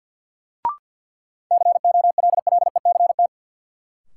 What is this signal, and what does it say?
Morse code, heard as a keyed tone, about 46 wpm, 700 Hz. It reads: COLLECT
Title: Sentences from Top 1600 Words - New Word Drill Alt - 50wpm